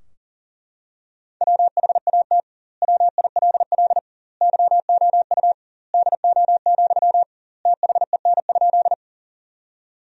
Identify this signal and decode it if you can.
Morse code, heard as a keyed tone, about 40 wpm, 700 Hz. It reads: WHAT WILL YOU DO, THEN?
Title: 2x Word-Spacing - Sentences from Top 100 Words - 40wpm